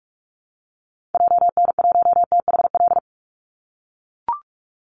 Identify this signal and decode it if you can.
Morse code, heard as a keyed tone, about 44 wpm, 700 Hz. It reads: JN1THL
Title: Call Signs - 45wpm